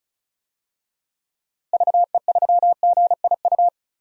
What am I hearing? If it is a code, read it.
Morse code, heard as a keyed tone, about 36 wpm, 700 Hz. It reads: VE3GIU
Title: DXCC-100 Call Signs - 6 Characters - Repeated 3x - 35wpm